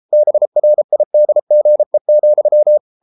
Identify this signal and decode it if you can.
Morse code, heard as a keyed tone, about 33 wpm, 600 Hz. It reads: BRIDGE,